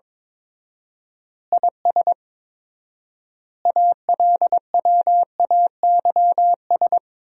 Morse code, heard as a keyed tone, about 22 wpm, 700 Hz. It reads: IS ALWAYS
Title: 4x Word-Spacing - Sentences from Top 400 Words - 22wpm